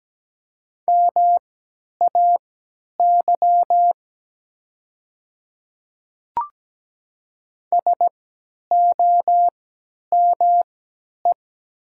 Morse code, heard as a keyed tone, about 17 wpm, spaced as for 10 wpm, 700 Hz. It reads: MAY SOME
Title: 3x Character Spacing - Top 100 Words - 17wpm